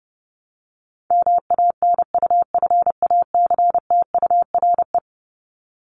Morse code, heard as a keyed tone, about 29 wpm, 700 Hz. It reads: MANUFACTURE
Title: ING Suffix - Top 100 Words with Base Words 30wpm